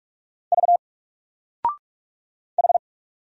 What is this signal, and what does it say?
Morse code, heard as a keyed tone, about 44 wpm, 700 Hz. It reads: V H
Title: Single Letter-Number 45wpm